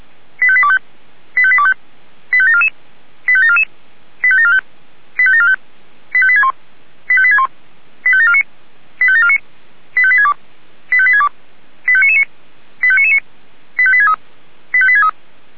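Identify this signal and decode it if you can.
Five-tone selective call (ZVEI), heard as quick runs of five tones, tones of about 70 ms. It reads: 85825 85825 85744 85744 85745 85745 85861 85861 85749 85749 85862 85862 85999 85999 85863 85863
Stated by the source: ZVEI five-tone selective call received on SDR.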